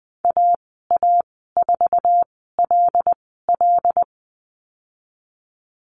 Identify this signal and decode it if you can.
Morse code, heard as a keyed tone, about 19 wpm, spaced as for 14 wpm, 700 Hz. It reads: AA4LL
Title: 088 Lesson 15g - Number 5 - Call Signs 20wpm